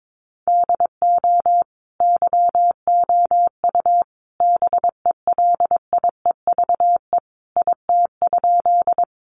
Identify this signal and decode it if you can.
Morse code, heard as a keyed tone, about 22 wpm, 700 Hz. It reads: DO YOU BELIEVE IT?
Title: Sentences from Top 1000 Words 22wpm